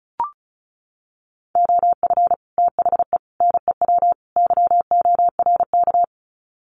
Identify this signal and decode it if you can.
Morse code, heard as a keyed tone, about 34 wpm, 700 Hz. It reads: OF THE NEW YORK
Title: Sets of 4 Words 35wpm